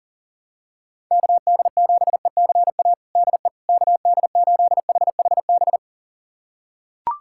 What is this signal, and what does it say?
Morse code, heard as a keyed tone, about 40 wpm, 700 Hz. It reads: KD7ECA DE KD8HHB
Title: USA Callsign DE Callsign - 6 Characters - 40wpm